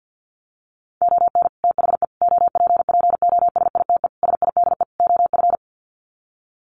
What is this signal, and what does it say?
Morse code, heard as a keyed tone, about 50 wpm, 700 Hz. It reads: ON THE OPPOSITE SIDE OF
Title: Sets of 5 Words Encore2 50wpm